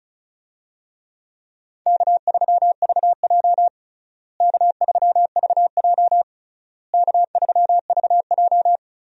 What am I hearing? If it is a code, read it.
Morse code, heard as a keyed tone, about 36 wpm, 700 Hz. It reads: K3VJ K3VJ K3VJ
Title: USA Call Signs - 4 Characters - Repeated 3x - 35wpm